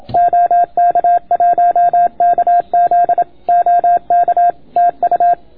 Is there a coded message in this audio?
Yes: Morse code from 0.2 to 5.3 s, heard as a keyed tone, about 26 wpm, 690 Hz. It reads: OK1KZ OK TU